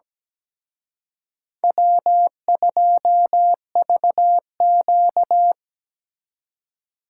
Morse code, heard as a keyed tone, about 20 wpm, 700 Hz. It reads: W2VQ
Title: Speed Racing - USA Call Signs 17wpm